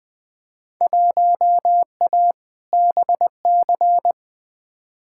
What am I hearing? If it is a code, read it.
Morse code, heard as a keyed tone, about 20 wpm, 700 Hz. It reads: 1A BC
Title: Field Day Exchanges - 20wpm